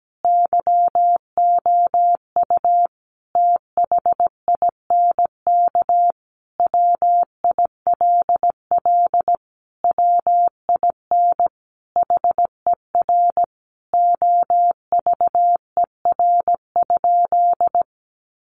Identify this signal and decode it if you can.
Morse code, heard as a keyed tone, about 17 wpm, 700 Hz. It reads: YOU THINK WILL WIN HER OVER?
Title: Sentences from Top 1500 Words 17wpm